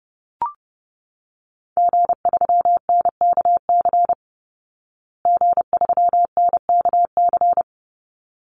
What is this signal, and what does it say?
Morse code, heard as a keyed tone, about 30 wpm, 700 Hz. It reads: G3NKC G3NKC
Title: Call Signs Repeated 3x - 30wpm